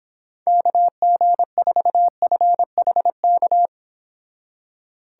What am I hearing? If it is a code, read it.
Morse code, heard as a keyed tone, about 26 wpm, 700 Hz. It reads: KG4FHK